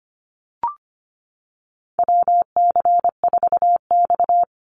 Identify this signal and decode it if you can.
Morse code, heard as a keyed tone, about 25 wpm, 700 Hz. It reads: WC4X
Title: Call Signs Repeated 2x - 25wpm